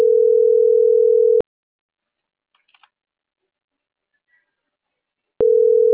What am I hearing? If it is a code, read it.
Ringback tone (North America), 440 + 480 Hz.
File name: CENTRAL NMP / out isabelponciano